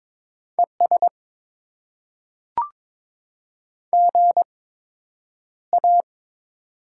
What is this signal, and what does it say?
Morse code, heard as a keyed tone, about 23 wpm, spaced as for 9 wpm, 700 Hz. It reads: ES GA
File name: Top 20 QSO Elements - 8x Character Spacing - 22wpm